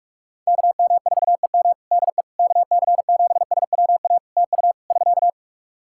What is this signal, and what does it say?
Morse code, heard as a keyed tone, about 44 wpm, 700 Hz. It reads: KM4EM DE KC7SWA TU <SK>
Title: Practice QSO - Exchange 4 - Ending - 45wpm